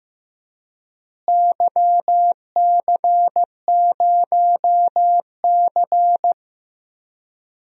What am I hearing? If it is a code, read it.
Morse code, heard as a keyed tone, about 15 wpm, 700 Hz. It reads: YC0C
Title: DXCC-100 Call Signs - 4 Characters - Repeated 2x - 15wpm